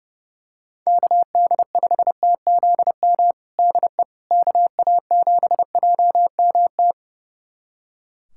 Morse code, heard as a keyed tone, about 30 wpm, 700 Hz. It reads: KD5TZM DE KA7JMT
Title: USA Callsign DE Callsign 30wpm